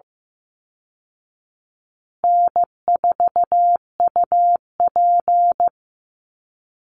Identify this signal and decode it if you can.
Morse code, heard as a keyed tone, about 15 wpm, 700 Hz. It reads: N4UP
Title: Call Signs Repeated 2x - 15wpm